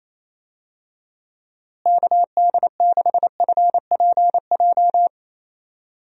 Morse code, heard as a keyed tone, about 27 wpm, 700 Hz. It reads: KD6FPJ
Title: USA Call Signs - 6 Characters - Repeated 3x - 28wpm